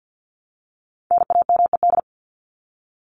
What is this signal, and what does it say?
Morse code, heard as a keyed tone, about 50 wpm, 700 Hz. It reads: NAMED
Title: ED Suffix - Top 100 Words with Base Words 50wpm